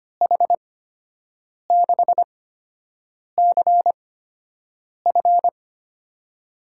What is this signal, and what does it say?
Morse code, heard as a keyed tone, about 25 wpm, spaced as for 7 wpm, 700 Hz. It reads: H6CF
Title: USA Call Signs - 5 Characters - 8x Character Spacing - 25wpm